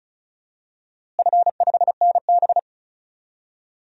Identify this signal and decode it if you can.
Morse code, heard as a keyed tone, about 36 wpm, 700 Hz. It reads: F5NB
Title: DXCC-100 Call Signs - 4 Characters - Repeated 2x - 35wpm